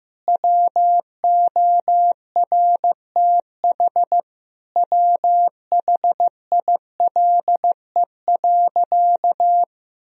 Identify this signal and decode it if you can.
Morse code, heard as a keyed tone, about 16 wpm, 700 Hz. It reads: WORTH WHILE.
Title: Sentences from Top 1000 Words - Review Part 1 - 15wpm